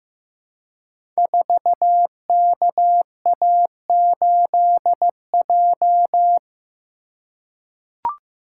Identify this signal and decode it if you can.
Morse code, heard as a keyed tone, about 15 wpm, 700 Hz. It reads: 4KA8J
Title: Code Groups - 5 Letter:Numbers - 15wpm